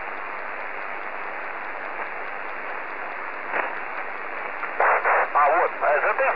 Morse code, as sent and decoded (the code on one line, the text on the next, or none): none